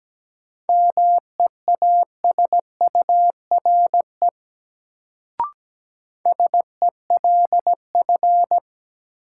Morse code, heard as a keyed tone, about 17 wpm, 700 Hz. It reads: MEASURE SELF
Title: Sentences from Top 1000 Words - Review Part 1 - New Word Drill - 17wpm